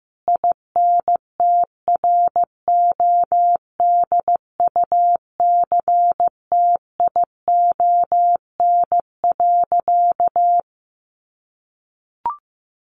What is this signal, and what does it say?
Morse code, heard as a keyed tone, about 14 wpm, 700 Hz. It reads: INTRODUCTION.
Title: Sentences from Top 1000 Words 15wpm